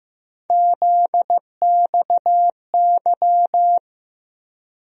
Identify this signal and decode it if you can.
Morse code, heard as a keyed tone, about 15 wpm, 700 Hz. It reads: ZXY
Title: USA Call Sign Postfixes - 3 Letters 15wpm